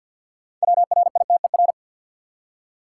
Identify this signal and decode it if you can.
Morse code, heard as a keyed tone, about 48 wpm, 700 Hz. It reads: WRITER
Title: Six Letter Words - 50wpm